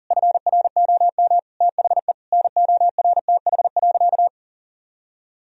Morse code, heard as a keyed tone, about 37 wpm, 700 Hz. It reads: FROM THE NORTH.
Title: Sentences from Top 1000 Words - Review Part 2 - 40wpm